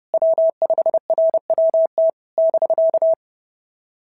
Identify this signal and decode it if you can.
Morse code, heard as a keyed tone, about 30 wpm, 650 Hz. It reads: W5RWT <BK>
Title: SOTA QSO Practice 30wpm